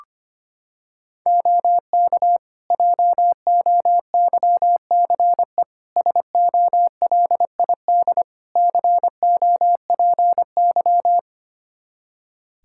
Morse code, heard as a keyed tone, about 25 wpm, 700 Hz. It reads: OK JOYCE SOLID COPY
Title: Practice QSO - Exchange 3 - More Info - 25wpm